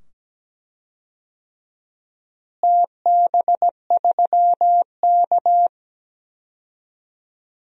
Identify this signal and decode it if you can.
Morse code, heard as a keyed tone, about 17 wpm, 700 Hz. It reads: TB3K
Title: DXCC-100 Call Signs - 4 Characters - Repeated 2x - 17wpm